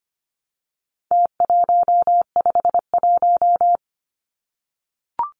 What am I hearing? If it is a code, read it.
Morse code, heard as a keyed tone, about 25 wpm, 700 Hz. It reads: T151
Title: Numbers - Contest Style - 4-Digit 25wpm